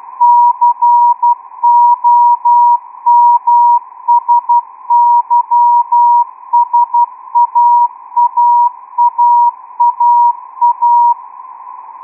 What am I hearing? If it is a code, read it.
Morse code, heard as a keyed tone, about 12 wpm, 960 Hz. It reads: COMSYSAAAAA
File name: ITF2-XQUAD-AIRSPY-04